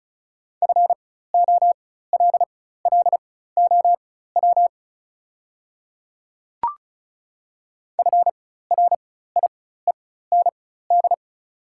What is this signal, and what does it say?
Morse code, heard as a keyed tone, about 36 wpm, spaced as for 16 wpm, 700 Hz. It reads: FOLLOW FRIEND
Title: 4x Character Spacing - Top 300 Words - 35wpm